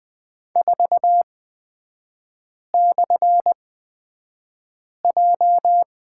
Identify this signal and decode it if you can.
Morse code, heard as a keyed tone, about 20 wpm, spaced as for 5 wpm, 700 Hz. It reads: 4/J
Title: 250 Lesson 40e - BK - Three Characters 20wpm